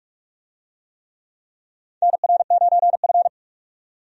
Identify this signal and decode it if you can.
Morse code, heard as a keyed tone, about 41 wpm, 700 Hz. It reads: NR9F
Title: USA Call Signs - 4 Characters - 45wpm